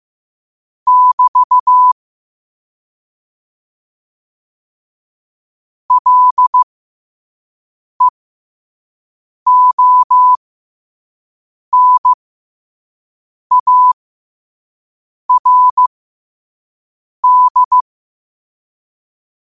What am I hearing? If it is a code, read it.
Morse code, heard as a keyed tone, about 15 wpm, spaced as for 5 wpm, 1000 Hz. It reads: <BT> LEONARD
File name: qotdfiles / 5wpm